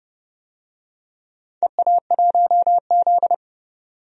Morse code, heard as a keyed tone, about 30 wpm, 700 Hz. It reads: EA1Z
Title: DXCC-100 Call Signs - 4 Characters - 30wpm